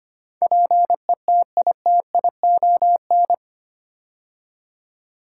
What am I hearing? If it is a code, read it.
Morse code, heard as a keyed tone, about 25 wpm, 700 Hz. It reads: PETITION
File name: 8 Letter Words - First 3 Letters Given - 25wpm